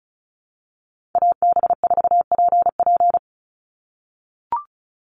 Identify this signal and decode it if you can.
Morse code, heard as a keyed tone, about 34 wpm, 700 Hz. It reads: AB4PP
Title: NA SOTA Chaser Callsigns 35wpm